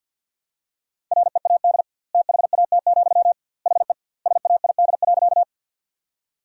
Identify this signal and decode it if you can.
Morse code, heard as a keyed tone, about 50 wpm, 700 Hz. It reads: READ THAT, HE SAID.
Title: 2x Word-Spacing - Sentences from Top 200 Words - 50wpm